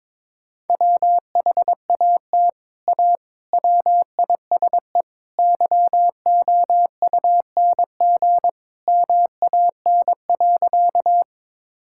Morse code, heard as a keyed tone, about 21 wpm, 700 Hz. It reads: WHAT A WISE YOUNG MAN.
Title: Sentences from Top 1600 Words 22wpm